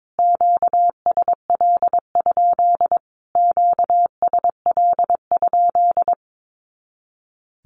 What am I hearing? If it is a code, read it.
Morse code, heard as a keyed tone, about 22 wpm, 700 Hz. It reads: QSL? QSL?
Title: Q-Codes Repeated 3x 22wpm